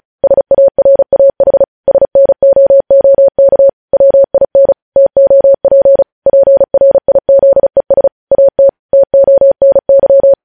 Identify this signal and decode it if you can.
Morse code, heard as a keyed tone, about 36 wpm, 550 Hz. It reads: SARAH SNOOK WIN TOP PRIZES AT TONY
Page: news headlines in morse code at 35 wpm